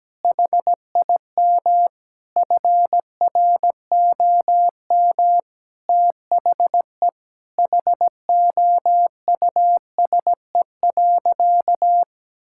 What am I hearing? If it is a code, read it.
Morse code, heard as a keyed tone, about 17 wpm, 700 Hz. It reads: HIM FROM THE HOUSE.
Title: Sentences from Top 1600 Words 17wpm